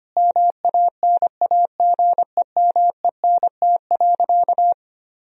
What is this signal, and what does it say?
Morse code, heard as a keyed tone, about 25 wpm, 700 Hz. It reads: MANAGEMENT.
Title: Sentences from Top 1000 Words - Review Part 4 - 25wpm